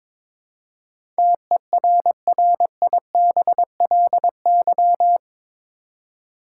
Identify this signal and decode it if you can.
Morse code, heard as a keyed tone, about 21 wpm, 700 Hz. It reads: TERRIBLY
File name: practice / 22wpm / Eight Letter Words - 22wpm